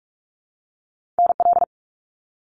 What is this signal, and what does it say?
Morse code, heard as a keyed tone, about 46 wpm, 700 Hz. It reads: NL
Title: Canadian Provinces and Territories - Repeated 3x - 45wpm